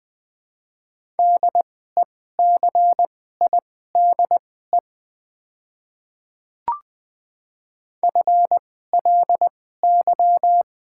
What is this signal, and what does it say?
Morse code, heard as a keyed tone, about 20 wpm, spaced as for 15 wpm, 700 Hz. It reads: DECIDE FLY
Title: 2x Character Spacing - Top 500 Words - 20wpm